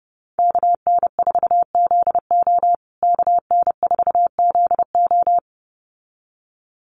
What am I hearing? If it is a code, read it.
Morse code, heard as a keyed tone, about 30 wpm, 700 Hz. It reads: KN4ZO KN4ZO
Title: USA Call Signs Repeated 2x 30wpm